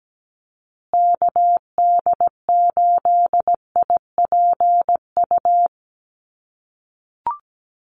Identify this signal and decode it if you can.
Morse code, heard as a keyed tone, about 17 wpm, 700 Hz. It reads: KD8IPU
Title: USA Call Signs Repeated 2x 17wpm